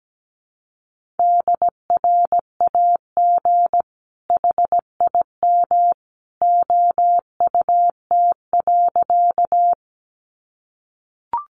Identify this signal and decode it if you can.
Morse code, heard as a keyed tone, about 17 wpm, 700 Hz. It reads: DRAG HIM OUT.
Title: Sentences from Top 1500 Words 17wpm